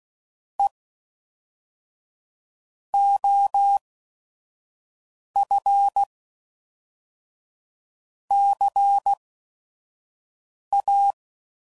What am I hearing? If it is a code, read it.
Morse code, heard as a keyed tone, about 16 wpm, spaced as for 5 wpm, 800 Hz. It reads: E OF CA
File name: CW Code Groups @5WPM / Technical Subjects / XC